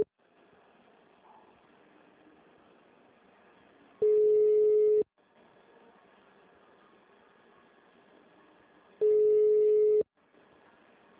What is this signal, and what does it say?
Ringback tone (Europe, 425 Hz).
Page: Brasilia - DF